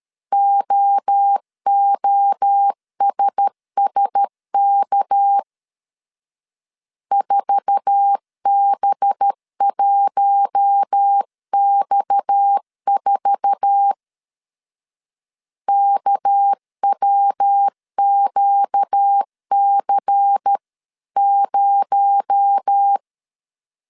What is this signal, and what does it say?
Morse code, heard as a keyed tone, about 13 wpm, 800 Hz. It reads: OOSSK 4B1X4 KWQC0
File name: CW / DET / 1 9 1500C S C 12wpm 3811